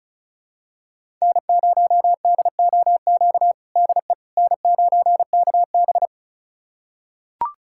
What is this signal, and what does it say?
Morse code, heard as a keyed tone, about 34 wpm, 700 Hz. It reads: N0DOQ DE N9KB
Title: USA Callsign DE Callsign 35wpm